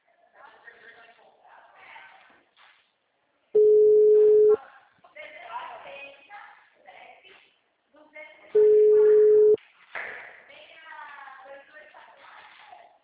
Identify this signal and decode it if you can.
Ringback tone (Europe, 425 Hz), with speech.